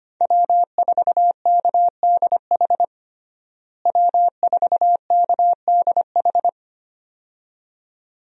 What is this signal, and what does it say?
Morse code, heard as a keyed tone, about 25 wpm, 700 Hz. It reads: W4KDH W4KDH
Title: USA Call Signs - 5 Characters - Repeated 2x - 25wpm